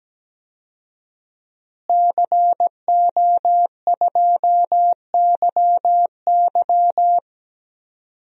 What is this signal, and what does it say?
Morse code, heard as a keyed tone, about 17 wpm, 700 Hz. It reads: CO2YY
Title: DXCC-100 Call Signs - 5 Characters - Repeated 3x - 17wpm